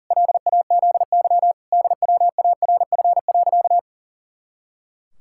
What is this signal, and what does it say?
Morse code, heard as a keyed tone, about 40 wpm, 700 Hz. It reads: LAZY DWARF.
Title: Pangram Challenge v2 - 40wpm